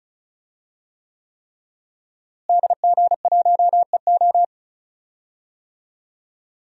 Morse code, heard as a keyed tone, about 36 wpm, 700 Hz. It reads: DG1EO
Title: DXCC-100 Call Signs - 5 Characters - Repeated 3x - 35wpm